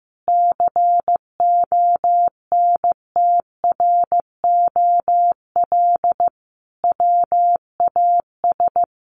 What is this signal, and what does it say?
Morse code, heard as a keyed tone, about 15 wpm, 700 Hz. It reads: CONTROL WAS
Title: Sentences from Top 1300 Words 15wpm